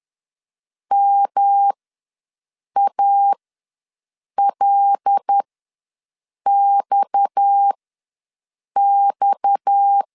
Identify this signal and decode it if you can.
Morse code, heard as a keyed tone, about 10 wpm, spaced as for 6 wpm, 800 Hz. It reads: MALXX
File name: LEZIONI / 6 PKX0 / 6 PKX ws 10wpm 3706